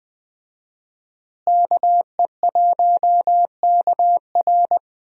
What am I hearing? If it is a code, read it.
Morse code, heard as a keyed tone, about 21 wpm, 700 Hz. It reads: KE1KR
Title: USA Call Signs - 5 Characters - Repeated 2x - 20wpm